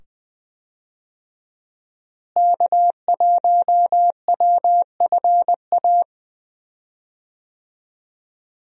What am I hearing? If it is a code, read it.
Morse code, heard as a keyed tone, about 21 wpm, 700 Hz. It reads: K1WFA
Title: USA Call Signs - 5 Characters - Repeated 2x - 20wpm